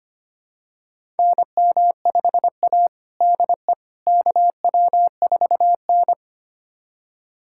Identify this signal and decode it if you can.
Morse code, heard as a keyed tone, about 25 wpm, 700 Hz. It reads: NM5A DE KW4N
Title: USA Callsign DE Callsign - 4 Characters - 25wpm